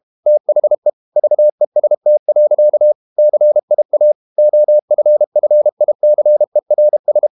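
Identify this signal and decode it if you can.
Morse code, heard as a keyed tone, about 32 wpm, 600 Hz. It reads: THE VEST. CIA OFFICERS
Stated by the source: Morse code practice at 32 wpm from CW med Gnister, reading Ken Follett's novel Never in English.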